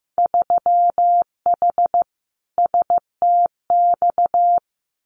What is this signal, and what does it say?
Morse code, heard as a keyed tone, about 15 wpm, 700 Hz. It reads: 3H STX
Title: Winter Field Day Exchanges Encore 15wpm